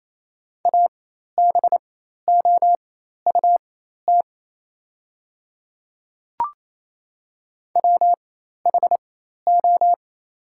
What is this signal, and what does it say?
Morse code, heard as a keyed tone, about 28 wpm, spaced as for 13 wpm, 700 Hz. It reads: ABOUT WHO
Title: 4x Character Spacing - Top 100 Words - 28wpm